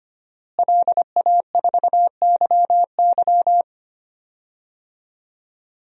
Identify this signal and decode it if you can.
Morse code, heard as a keyed tone, about 25 wpm, 700 Hz. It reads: LA4YY
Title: DXCC-100 Call Signs - 5 Characters - Repeated 2x - 25wpm